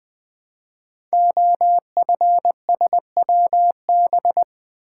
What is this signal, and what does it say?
Morse code, heard as a keyed tone, about 20 wpm, 700 Hz. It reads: OFSWB
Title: Code Groups - 5 Letters - 20wpm